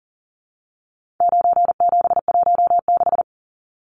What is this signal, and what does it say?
Morse code, heard as a keyed tone, about 40 wpm, 700 Hz. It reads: 9716
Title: Numbers - 4-Digit 40wpm